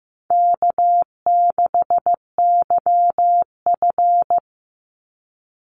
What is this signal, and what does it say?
Morse code, heard as a keyed tone, about 15 wpm, 700 Hz. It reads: K6YF
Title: Easy Call Signs Repeated 2x - 15wpm